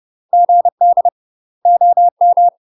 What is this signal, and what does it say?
Morse code, heard as a keyed tone, about 30 wpm, 700 Hz. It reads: GD OM
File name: UK MASTER QSOs TRAINING / 30WPM / UK MASTER QSOs 0050